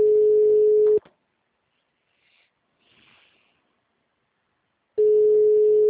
Ringback tone (Europe, 425 Hz).